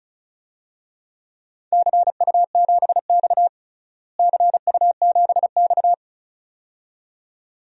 Morse code, heard as a keyed tone, about 36 wpm, 700 Hz. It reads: CU7X CU7X
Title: DXCC-100 Call Signs - 4 Characters - Repeated 2x - 35wpm